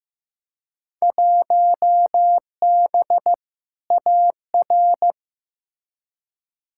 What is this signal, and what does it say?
Morse code, heard as a keyed tone, about 15 wpm, 700 Hz. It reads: 1B AR
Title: Field Day Exchanges Encore - 15wpm